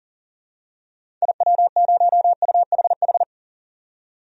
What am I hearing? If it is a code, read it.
Morse code, heard as a keyed tone, about 40 wpm, 700 Hz. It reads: IW0UHH
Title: DXCC-100 Call Signs - 6 Characters - 40wpm